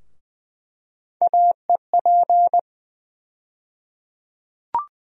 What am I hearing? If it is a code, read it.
Morse code, heard as a keyed tone, about 20 wpm, 700 Hz. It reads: AEP
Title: USA Call Sign Postfixes - 3 Letters 20wpm